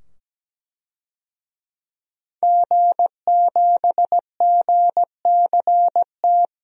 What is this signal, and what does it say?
Morse code, heard as a keyed tone, about 17 wpm, 700 Hz. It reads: G7GCT
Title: DXCC-100 Call Signs - 5 Characters - 17wpm